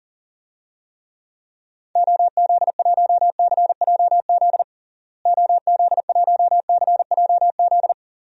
Morse code, heard as a keyed tone, about 37 wpm, 700 Hz. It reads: OZ1CJZ OZ1CJZ
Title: DXCC-100 Call Signs - 6 Characters - Repeated 3x - 40wpm